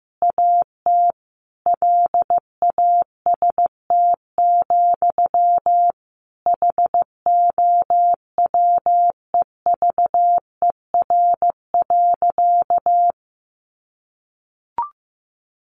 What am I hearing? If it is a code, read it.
Morse code, heard as a keyed tone, about 14 wpm, 700 Hz. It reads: AT LAST, HOWEVER.
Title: Sentences from Top 600 Words 15wpm